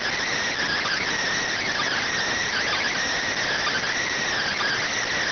Emergency Alert System (SAME) tones (header bursts), heard as harsh buzzing bursts. It is heard from 0.6 to 4.9 s.
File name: findings / CT CORONAL 710mm 10mm scout.gen-35 rx